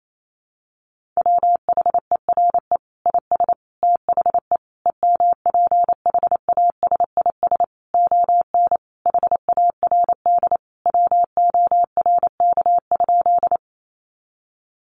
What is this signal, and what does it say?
Morse code, heard as a keyed tone, about 28 wpm, 700 Hz. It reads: WHERE IS THE EMPHASIS ON HARD WORK?
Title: Sentences from Top 1400 Words 28wpm